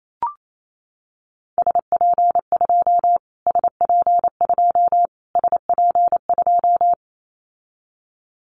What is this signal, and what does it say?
Morse code, heard as a keyed tone, about 28 wpm, 700 Hz. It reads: SP2 SP2 SP2
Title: Callsign Prefixes for Top 25 DXCC - Repeated 3x - 28wpm